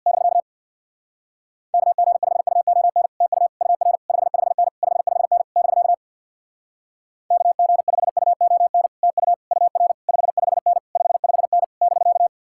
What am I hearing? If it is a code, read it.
Morse code, heard as a keyed tone, about 64 wpm, 700 Hz. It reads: <BK> KG5UON TU UR 55N 55N <BK> KG5UON TU UR 55N 55N <BK>